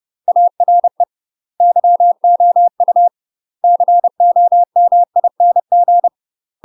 Morse code, heard as a keyed tone, about 30 wpm, 700 Hz. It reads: ARE YOU COMING
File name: UK MASTER QSOs TRAINING / 30WPM / UK MASTER QSOs 0021